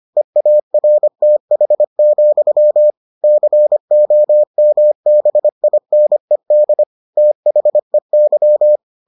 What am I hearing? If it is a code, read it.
Morse code, heard as a keyed tone, about 25 wpm, 600 Hz. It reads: EARTH, COMBINED THEY